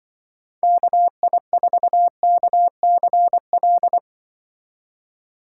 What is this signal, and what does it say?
Morse code, heard as a keyed tone, about 24 wpm, 700 Hz. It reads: KI4KCL
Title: Speed Racing - USA Call Signs 20wpm